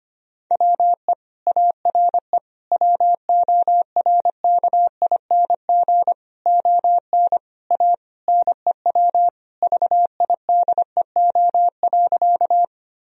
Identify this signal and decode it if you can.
Morse code, heard as a keyed tone, about 25 wpm, 700 Hz. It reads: WE ARE WORKING ON A NEW VIDEO.
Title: Sentences from Top 1000 Words - Review Part 4 - 25wpm